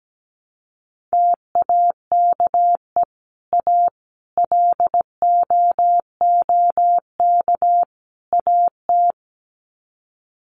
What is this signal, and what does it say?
Morse code, heard as a keyed tone, about 17 wpm, 700 Hz. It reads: TAKE A LOOK AT
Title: Sets of 4 Words 17wpm